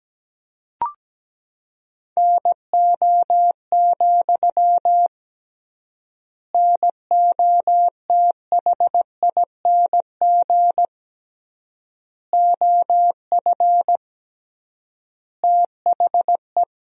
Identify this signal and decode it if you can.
Morse code, heard as a keyed tone, about 17 wpm, 700 Hz. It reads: NO, NOTHING OF THE
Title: 3x Word-Spacing - Sentences from Top 500 Words - 17wpm